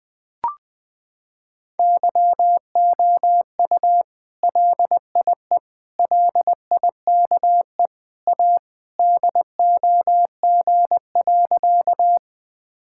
Morse code, heard as a keyed tone, about 20 wpm, 700 Hz. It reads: YOU LIE LIKE A DOG.
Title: Sentences from Top 1600 Words 20wpm